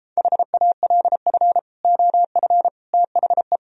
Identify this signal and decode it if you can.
Morse code, heard as a keyed tone, about 35 wpm, 700 Hz. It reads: HALF OF THE